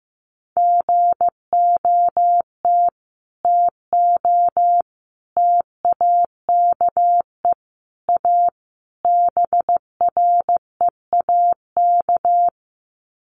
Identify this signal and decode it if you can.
Morse code, heard as a keyed tone, about 15 wpm, 700 Hz. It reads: GOT TO TAKE A BREAK
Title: Sets of 5 Words Encore 15wpm